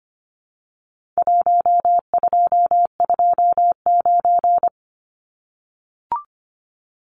Morse code, heard as a keyed tone, about 25 wpm, 700 Hz. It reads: 1229
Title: Numbers - Contest Style - 4-Digit 25wpm